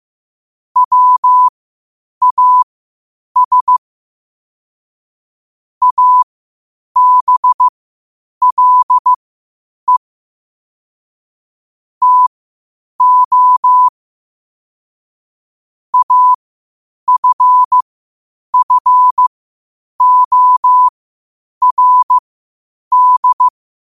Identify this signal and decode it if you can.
Morse code, heard as a keyed tone, about 15 wpm, spaced as for 8 wpm, 1000 Hz. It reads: WAS ABLE TO AFFORD
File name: qotdfiles / 8wpm